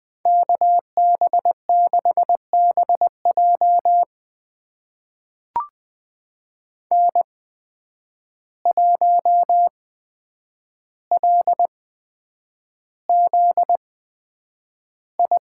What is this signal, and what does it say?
Morse code, heard as a keyed tone, about 20 wpm, spaced as for 9 wpm, 700 Hz. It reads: KB6BJ N1LZI
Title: USA Call Signs - 5 Characters - 8x Character Spacing - 20wpm